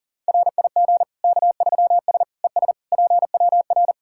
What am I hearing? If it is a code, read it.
Morse code, heard as a keyed tone, about 40 wpm, 700 Hz. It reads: RIG K3S ES PWR
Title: Practice QSO - Exchange 2 - Info - 40wpm